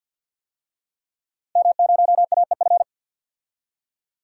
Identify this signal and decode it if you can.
Morse code, heard as a keyed tone, about 52 wpm, 700 Hz. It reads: M0AEF
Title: DXCC-100 Call Signs - 5 Characters - Repeated 2x - 50wpm